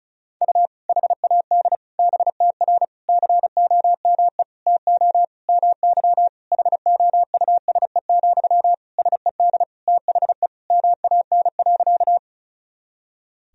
Morse code, heard as a keyed tone, about 36 wpm, 700 Hz. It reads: U HAD BTR COME TO MY HOUSE, SED THE MAN.
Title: Sentences from Top 400 Words and Abbreviations 35wpm